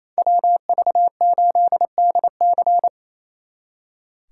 Morse code, heard as a keyed tone, about 34 wpm, 700 Hz. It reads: WV8DC
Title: Speed Racing - USA 2x2 Call Signs 28wpm